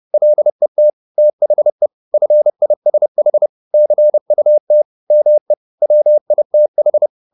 Morse code, heard as a keyed tone, about 30 wpm, 600 Hz. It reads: LET THE FISH CUT ME WITH
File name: the old man and the sea / the old man and the sea 1 30wpm